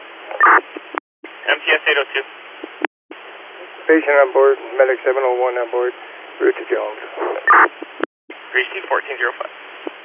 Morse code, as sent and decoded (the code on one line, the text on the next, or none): none